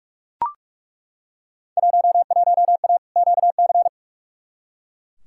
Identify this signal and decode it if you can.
Morse code, heard as a keyed tone, about 42 wpm, 700 Hz. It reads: 11A QC
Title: Field Day Exchanges Encore - 45wpm